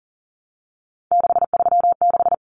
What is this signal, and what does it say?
Morse code, heard as a keyed tone, about 40 wpm, 700 Hz. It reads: 636
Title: Numbers - 3-Digit 40wpm